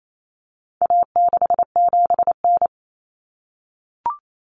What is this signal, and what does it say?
Morse code, heard as a keyed tone, about 28 wpm, 700 Hz. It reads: A67N
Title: Common Cut Numbers - 4-Digit 28wpm